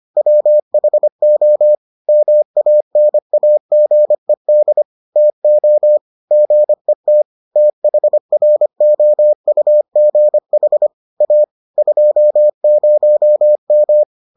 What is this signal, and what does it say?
Morse code, heard as a keyed tone, about 25 wpm, 600 Hz. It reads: WHO MANAGED TO GET THROUGH A 20M